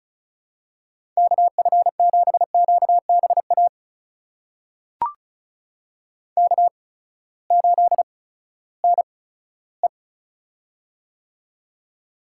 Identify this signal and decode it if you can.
Morse code, heard as a keyed tone, about 36 wpm, spaced as for 11 wpm, 700 Hz. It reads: KF7QB AK8NE
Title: USA Call Signs - 8x Character Spacing - 35wpm